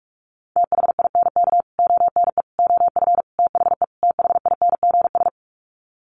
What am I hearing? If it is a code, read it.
Morse code, heard as a keyed tone, about 44 wpm, 700 Hz. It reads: THINK ONE OF THE THINGS
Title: Sets of 5 Words Encore2 45wpm